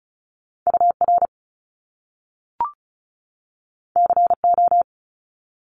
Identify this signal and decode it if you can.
Morse code, heard as a keyed tone, about 36 wpm, 700 Hz. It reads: UR CO
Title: Top Two Letter Combinations in Words 35wpm